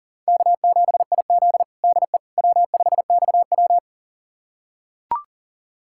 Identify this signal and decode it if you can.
Morse code, heard as a keyed tone, about 40 wpm, 700 Hz. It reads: K7IZ DE W5XW
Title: USA Callsign DE Callsign - 4 Characters - 40wpm